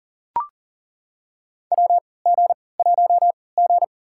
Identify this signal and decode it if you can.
Morse code, heard as a keyed tone, about 40 wpm, spaced as for 23 wpm, 700 Hz. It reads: WG1G
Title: USA Call Signs - 3x Character Spacing - 40wpm